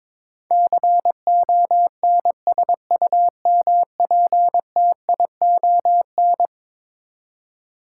Morse code, heard as a keyed tone, about 22 wpm, 700 Hz. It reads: CONSUMPTION
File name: practice / 22wpm / Eleven Letter Words - 22wpm